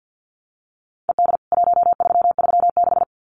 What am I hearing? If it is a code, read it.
Morse code, heard as a keyed tone, about 50 wpm, 700 Hz. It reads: ED 1336
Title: CWT Exchange 50wpm